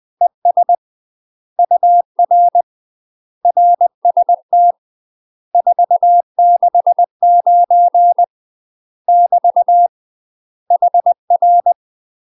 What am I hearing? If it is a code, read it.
Morse code, heard as a keyed tone, about 20 wpm, 700 Hz. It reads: ES UR RST 469 <BT> HR
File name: UK MASTER QSOs TRAINING / 20WPM / UK MASTER QSOs 0031